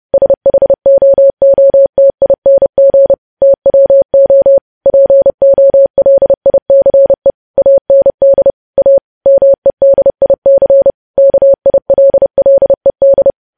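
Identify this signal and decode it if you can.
Morse code, heard as a keyed tone, about 30 wpm, 550 Hz. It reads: SHOOTING TWO POLICE AND A MEDIC KILLED
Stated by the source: News headlines sent as Morse practice at 30.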